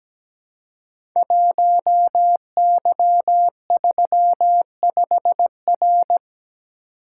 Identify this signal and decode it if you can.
Morse code, heard as a keyed tone, about 17 wpm, 700 Hz. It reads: 1Y35R
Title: Code Groups - 5 Letter:Numbers - 17wpm